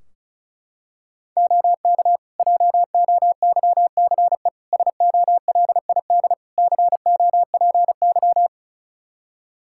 Morse code, heard as a keyed tone, about 36 wpm, 700 Hz. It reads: OK JOYCE SOLID COPY
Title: Practice QSO - Exchange 3 - More Info - 35wpm